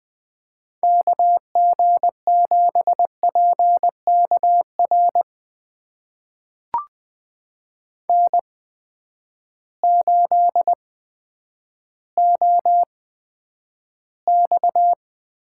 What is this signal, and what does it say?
Morse code, heard as a keyed tone, about 20 wpm, spaced as for 10 wpm, 700 Hz. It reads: KG7PKR N8OX
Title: USA Call Signs - 8x Character Spacing - 20wpm